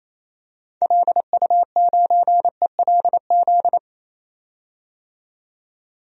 Morse code, heard as a keyed tone, about 28 wpm, 700 Hz. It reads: LU9ELZ
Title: DXCC-100 Call Signs - 6 Characters - 28wpm